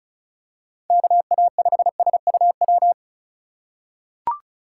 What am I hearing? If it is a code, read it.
Morse code, heard as a keyed tone, about 36 wpm, 700 Hz. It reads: KA5SUW
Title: USA Call Signs - 6 Characters - 2x Character Spacing - 35wpm